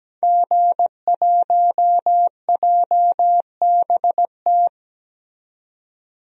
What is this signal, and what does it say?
Morse code, heard as a keyed tone, about 17 wpm, 700 Hz. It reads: G1JBT
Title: DXCC-100 Call Signs - 5 Characters - 17wpm